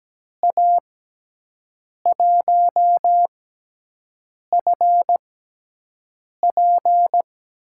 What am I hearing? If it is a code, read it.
Morse code, heard as a keyed tone, about 17 wpm, spaced as for 6 wpm, 700 Hz. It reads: A1FP
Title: USA Call Signs - 5 Characters - 6x Character Spacing - 17wpm